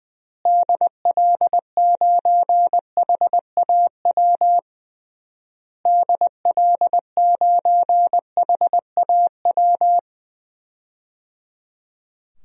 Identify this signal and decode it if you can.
Morse code, heard as a keyed tone, about 20 wpm, 700 Hz. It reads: DL9HAW DL9HAW
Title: DXCC-100 Call Signs - 6 Characters - Repeated 3x - 20wpm